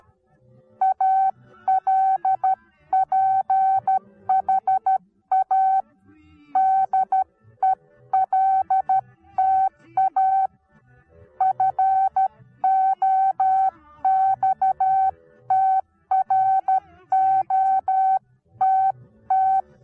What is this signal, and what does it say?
Morse code, heard as a keyed tone, about 13 wpm, 760 Hz. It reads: ALPHA DELTA FOXTROTT
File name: Onlinelehrgang / morsen / DJ4UF Morsen19